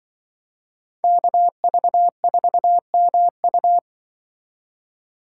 Morse code, heard as a keyed tone, about 25 wpm, 700 Hz. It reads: KV4MU
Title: Speed Racing - USA 2x2 Call Signs 20wpm